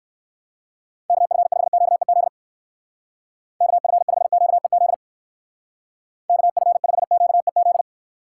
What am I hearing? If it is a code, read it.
Morse code, heard as a keyed tone, about 58 wpm, 700 Hz. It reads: KF5QEZ KF5QEZ KF5QEZ